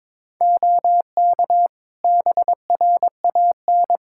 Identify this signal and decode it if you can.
Morse code, heard as a keyed tone, about 22 wpm, 700 Hz. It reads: OK BRAN
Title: Practice QSO - Exchange 3 - More Info - 22wpm